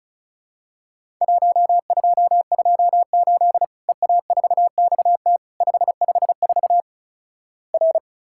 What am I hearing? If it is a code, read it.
Morse code, heard as a keyed tone, about 36 wpm, 700 Hz. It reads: 1228 EA4XT 554 R
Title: Worked All Europe DX Contest - QTC Exchange - 35wpm